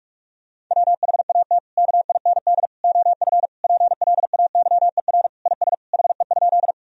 Morse code, heard as a keyed tone, about 46 wpm, 700 Hz. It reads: WHAT KIND OF PLAYER IS HE?
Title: Sentences from Top 1000 Words - Review Part 4 - 45wpm